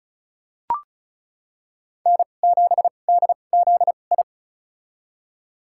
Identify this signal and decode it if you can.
Morse code, heard as a keyed tone, about 34 wpm, spaced as for 25 wpm, 700 Hz. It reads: N7DZI
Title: USA Call Signs - 5 Characters - 2x Character Spacing - 35wpm